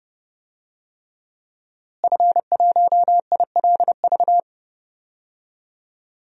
Morse code, heard as a keyed tone, about 30 wpm, 700 Hz. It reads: F1ILV
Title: DXCC-100 Call Signs - 5 Characters - Repeated 3x - 30wpm